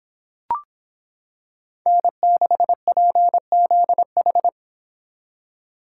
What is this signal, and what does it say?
Morse code, heard as a keyed tone, about 25 wpm, 700 Hz. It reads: N6PZH